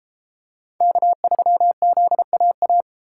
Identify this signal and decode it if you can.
Morse code, heard as a keyed tone, about 32 wpm, 700 Hz. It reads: K3ZAA